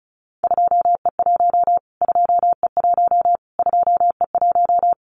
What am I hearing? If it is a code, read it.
Morse code, heard as a keyed tone, about 34 wpm, 700 Hz. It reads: 2E1 2E1 2E1
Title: Callsign Prefixes for Top 25 DXCC - Repeated 3x - 35wpm